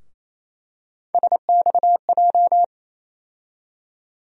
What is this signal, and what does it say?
Morse code, heard as a keyed tone, about 27 wpm, 700 Hz. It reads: SXJ
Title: USA Call Sign Postfixes - 3 Letters 28wpm